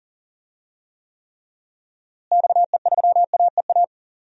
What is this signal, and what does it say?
Morse code, heard as a keyed tone, about 37 wpm, 700 Hz. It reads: XE3AEA
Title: DXCC-100 Call Signs - 6 Characters - Repeated 3x - 40wpm